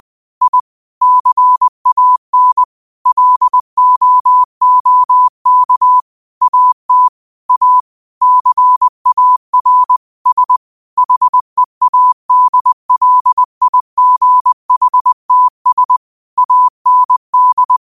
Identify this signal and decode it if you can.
Morse code, heard as a keyed tone, about 20 wpm, 1000 Hz. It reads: I CAN LOOK AT A CAR S HEADLIGHTS AND
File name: 20wpm